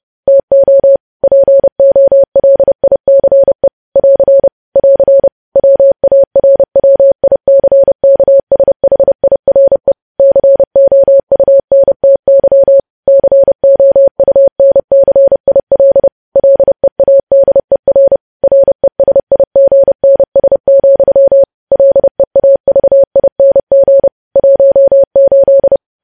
Morse code, heard as a keyed tone, about 30 wpm, 550 Hz. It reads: TO POLICE <AR> <AR> WARWICKSHIRE COUNTY COUNCIL LEADER RESIGNS, LEAVING 18
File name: podcast / News headlines-1x-30x30